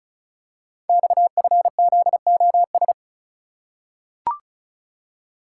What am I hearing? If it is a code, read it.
Morse code, heard as a keyed tone, about 36 wpm, 700 Hz. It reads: XFZOS
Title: Code Groups - 5 Letters - 35wpm